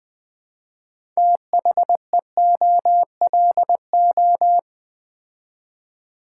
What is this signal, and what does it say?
Morse code, heard as a keyed tone, about 21 wpm, 700 Hz. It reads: THEOLO
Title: Build A Word - 3 Letters At A Time - 8-Letter Words - 20wpm